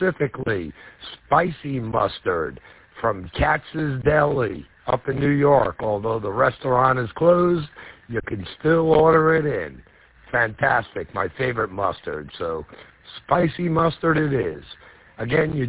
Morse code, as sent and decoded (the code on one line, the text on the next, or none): none